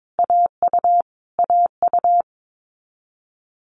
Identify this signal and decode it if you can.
Morse code, heard as a keyed tone, about 22 wpm, 700 Hz. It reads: AU AU
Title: Multi-letter Phonemes Repeated 3x 22wpm